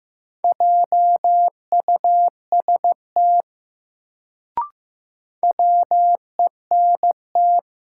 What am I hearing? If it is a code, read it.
Morse code, heard as a keyed tone, about 15 wpm, 700 Hz. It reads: JUST WENT
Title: Sentences from Top 200 Words - New Word Drill - 15wpm